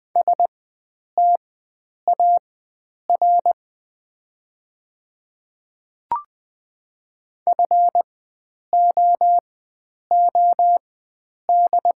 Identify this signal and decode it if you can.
Morse code, heard as a keyed tone, about 20 wpm, spaced as for 9 wpm, 700 Hz. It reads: STAR FOOD
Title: 4x Character Spacing - Top 500 Words - 20wpm